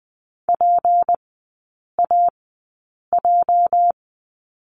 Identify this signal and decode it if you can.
Morse code, heard as a keyed tone, about 20 wpm, spaced as for 8 wpm, 700 Hz. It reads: PAJ
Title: 218 Lesson 35e - Number 0 - Three Characters 20wpm